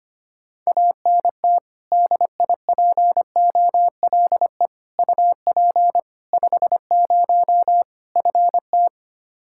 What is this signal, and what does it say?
Morse code, heard as a keyed tone, about 25 wpm, 700 Hz. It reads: ANT DIPOLE UP 50 FT
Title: K1USN SST QSO Practice 25wpm